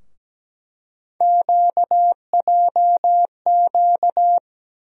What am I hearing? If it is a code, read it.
Morse code, heard as a keyed tone, about 17 wpm, 700 Hz. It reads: QJQ
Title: USA Call Sign Postfixes - 3 Letters 17wpm